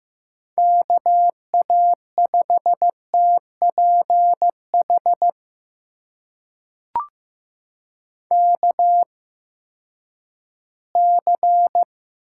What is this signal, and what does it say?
Morse code, heard as a keyed tone, about 15 wpm, spaced as for 10 wpm, 700 Hz. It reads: KA5TPH KC
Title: USA Call Signs - 6 Characters - 8x Character Spacing - 15wpm